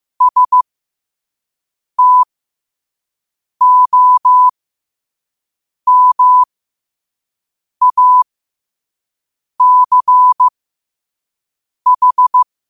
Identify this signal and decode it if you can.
Morse code, heard as a keyed tone, about 15 wpm, spaced as for 5 wpm, 1000 Hz. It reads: STOMACH